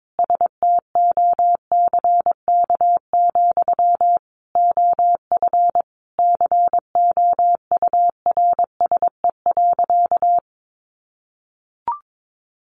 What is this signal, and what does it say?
Morse code, heard as a keyed tone, about 22 wpm, 700 Hz. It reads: STOCK, OF COURSE.
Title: Sentences from Top 800 Words 22wpm